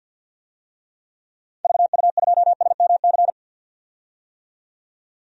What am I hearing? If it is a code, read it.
Morse code, heard as a keyed tone, about 48 wpm, 700 Hz. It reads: VU2SMC
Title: DXCC-100 Call Signs - 6 Characters - 50wpm